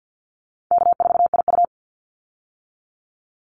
Morse code, heard as a keyed tone, about 50 wpm, 700 Hz. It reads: K4IU
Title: Easy Call Signs - 50wpm